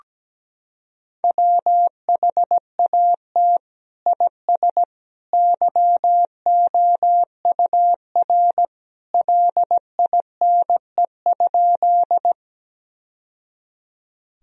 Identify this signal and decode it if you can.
Morse code, heard as a keyed tone, about 17 wpm, 700 Hz. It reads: WHAT IS YOUR LINE?
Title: Sentences from Top 1000 Words - Review Part 1 - 17wpm